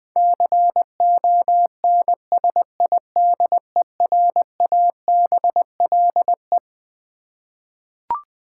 Morse code, heard as a keyed tone, about 20 wpm, 700 Hz. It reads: CONSIDERABLE
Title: Twelve Letter Words - 20wpm